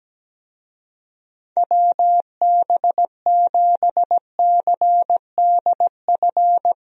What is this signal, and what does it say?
Morse code, heard as a keyed tone, about 17 wpm, 700 Hz. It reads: WB7CDF
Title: USA Call Signs - 6 Characters - Repeated 3x - 17wpm